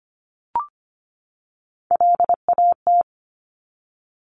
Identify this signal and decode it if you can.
Morse code, heard as a keyed tone, about 25 wpm, 700 Hz. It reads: LAT
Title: Top Three Letter Combinations in Words 25wpm